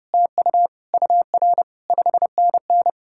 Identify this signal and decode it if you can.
Morse code, heard as a keyed tone, about 38 wpm, 700 Hz. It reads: TU UR 5NN